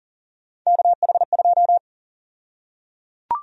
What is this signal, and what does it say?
Morse code, heard as a keyed tone, about 37 wpm, 700 Hz. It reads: KH2
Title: USA Call Sign Prefixes - 40wpm